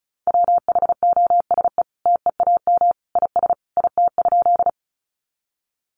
Morse code, heard as a keyed tone, about 34 wpm, 700 Hz. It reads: WHOSE TEAM IS IT?
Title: Sentences from Top 1500 Words 35wpm